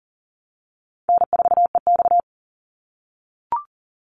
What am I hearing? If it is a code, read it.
Morse code, heard as a keyed tone, about 40 wpm, 700 Hz. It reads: N4EX
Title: NA SOTA Chaser Callsigns 40wpm